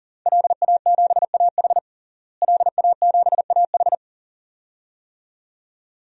Morse code, heard as a keyed tone, about 40 wpm, 700 Hz. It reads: LA7AH LA7AH
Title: DXCC-100 Call Signs - 5 Characters - Repeated 2x - 40wpm